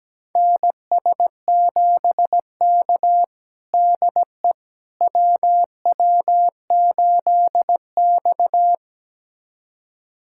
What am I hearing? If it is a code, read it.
Morse code, heard as a keyed tone, about 17 wpm, 700 Hz. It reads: NS7K DE WW8X
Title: USA Callsign DE Callsign - 4 Characters - 17wpm